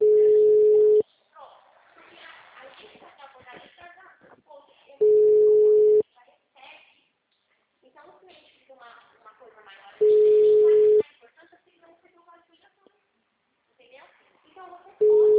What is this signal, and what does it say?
Ringback tone (Europe, 425 Hz), with speech.